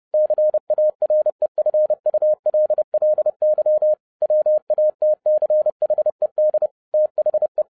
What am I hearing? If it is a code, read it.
Morse code, heard as a keyed tone, about 30 wpm, 610 Hz. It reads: CAREFULLY WATCHED THE
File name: 30wpm